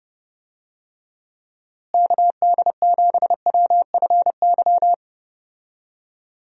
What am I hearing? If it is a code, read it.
Morse code, heard as a keyed tone, about 30 wpm, 700 Hz. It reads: KD7WFY
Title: USA Call Signs - 6 Characters - Repeated 2x - 30wpm